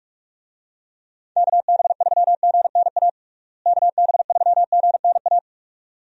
Morse code, heard as a keyed tone, about 46 wpm, 700 Hz. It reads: KB3GNA KB3GNA
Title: USA Call Signs - 6 Characters - Repeated 2x - 45wpm